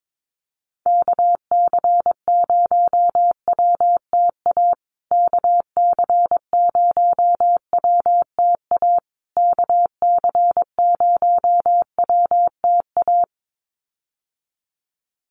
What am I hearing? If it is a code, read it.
Morse code, heard as a keyed tone, about 22 wpm, 700 Hz. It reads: KC0WTA KC0WTA KC0WTA
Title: USA Call Signs Repeated 3x 22wpm